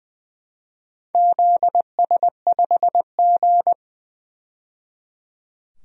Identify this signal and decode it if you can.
Morse code, heard as a keyed tone, about 20 wpm, 700 Hz. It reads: ZS5G
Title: DXCC-100 Call Signs - 4 Characters - 20wpm